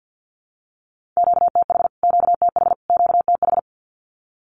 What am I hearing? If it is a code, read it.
Morse code, heard as a keyed tone, about 48 wpm, 700 Hz. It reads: QTH QTH QTH
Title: Q-Codes Repeated 3x 50wpm